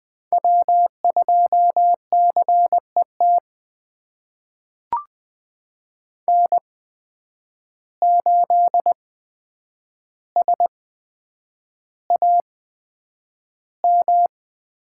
Morse code, heard as a keyed tone, about 20 wpm, spaced as for 9 wpm, 700 Hz. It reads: W2CET N8SAM
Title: USA Call Signs - 5 Characters - 8x Character Spacing - 20wpm